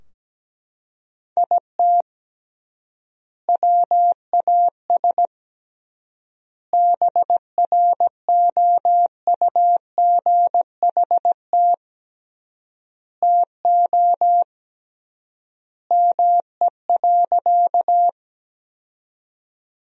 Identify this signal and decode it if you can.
Morse code, heard as a keyed tone, about 17 wpm, 700 Hz. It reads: IT WAS BROUGHT TO ME.
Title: 3x Word-Spacing - Sentences from Top 500 Words - 17wpm